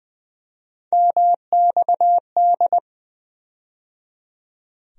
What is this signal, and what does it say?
Morse code, heard as a keyed tone, about 21 wpm, 700 Hz. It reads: MXD
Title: USA Call Sign Postfixes - 3 Letters 20wpm